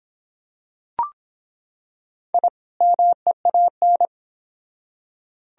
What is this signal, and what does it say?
Morse code, heard as a keyed tone, about 25 wpm, 700 Hz. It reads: I MEAN